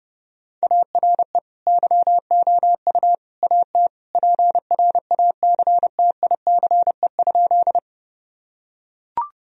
Morse code, heard as a keyed tone, about 32 wpm, 700 Hz. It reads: ARE YOU AT PRACTICE?
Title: Sentences from Top 1000 Words - Review Part 4 - 30wpm